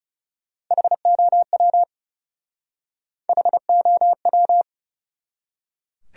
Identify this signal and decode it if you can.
Morse code, heard as a keyed tone, about 40 wpm, 700 Hz. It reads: HOW HOW